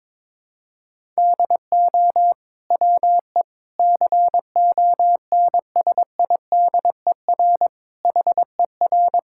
Morse code, heard as a keyed tone, about 22 wpm, 700 Hz. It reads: DO WE CONSIDER HER
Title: Sentences from Top 1000 Words - Review Part 5 - 22wpm